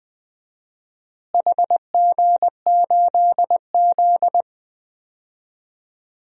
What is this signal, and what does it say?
Morse code, heard as a keyed tone, about 20 wpm, 700 Hz. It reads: HG8Z
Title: DXCC-100 Call Signs - 4 Characters - Repeated 3x - 20wpm